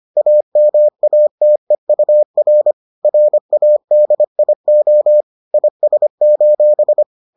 Morse code, heard as a keyed tone, about 25 wpm, 600 Hz. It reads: AMATEUR RADIO IS: